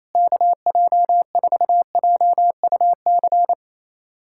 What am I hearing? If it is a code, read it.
Morse code, heard as a keyed tone, about 28 wpm, 700 Hz. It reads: KJ4JUC
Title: USA Callsign DE Callsign 28wpm